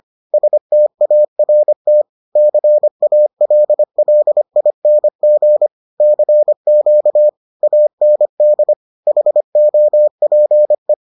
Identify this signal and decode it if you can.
Morse code, heard as a keyed tone, about 25 wpm, 600 Hz. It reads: START CALLING CQ AND HOPE